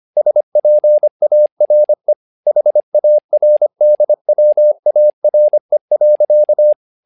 Morse code, heard as a keyed tone, about 25 wpm, 600 Hz. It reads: SPARE HARDWARE.
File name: learning center / tower trouble / tower trouble 25wpm